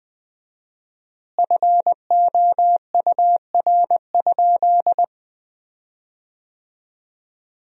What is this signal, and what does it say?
Morse code, heard as a keyed tone, about 21 wpm, 700 Hz. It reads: FOUR?
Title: 4x Word-Spacing - Sentences from Top 300 Words - 20wpm